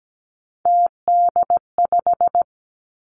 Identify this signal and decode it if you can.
Morse code, heard as a keyed tone, about 17 wpm, 700 Hz. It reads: TD5
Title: Three Challenging Letter-Number Combinations 17wpm